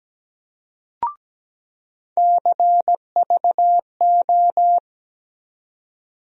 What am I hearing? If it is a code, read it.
Morse code, heard as a keyed tone, about 17 wpm, 700 Hz. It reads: CVO
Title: USA Call Sign Postfixes - 3 Letters 17wpm